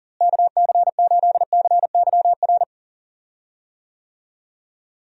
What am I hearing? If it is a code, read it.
Morse code, heard as a keyed tone, about 37 wpm, 700 Hz. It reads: KC8CYR
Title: USA Call Signs - 6 Characters - Repeated 3x - 40wpm